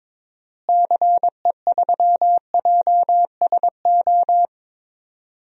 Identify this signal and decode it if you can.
Morse code, heard as a keyed tone, about 22 wpm, 700 Hz. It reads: CE3JSO
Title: DXCC-100 Call Signs - 6 Characters - Repeated 2x - 22wpm